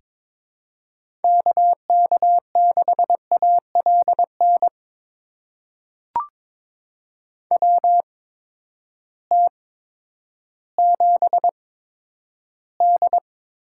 Morse code, heard as a keyed tone, about 21 wpm, spaced as for 11 wpm, 700 Hz. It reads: KK6ALN WT7D
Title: USA Call Signs - 8x Character Spacing - 22wpm